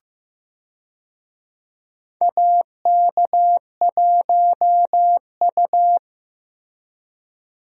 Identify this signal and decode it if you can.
Morse code, heard as a keyed tone, about 15 wpm, 700 Hz. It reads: AK1U
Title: USA Call Signs - 4 Characters - Repeated 2x - 15wpm